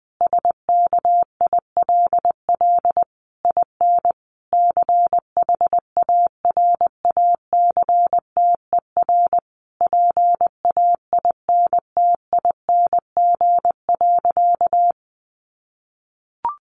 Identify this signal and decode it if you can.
Morse code, heard as a keyed tone, about 20 wpm, 700 Hz. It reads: SKILL IN CHARACTER PAINTING.
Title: Sentences from Top 1200 Words 20wpm